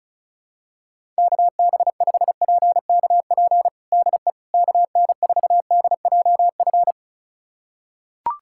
Morse code, heard as a keyed tone, about 36 wpm, 700 Hz. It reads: KB5PKP DE KN4DJF
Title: USA Callsign DE Callsign - 6 Characters - 35wpm